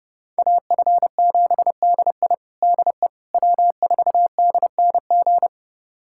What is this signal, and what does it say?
Morse code, heard as a keyed tone, about 30 wpm, 700 Hz. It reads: AF7DI DE W4DNG
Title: USA Callsign DE Callsign - 5 Characters - 30wpm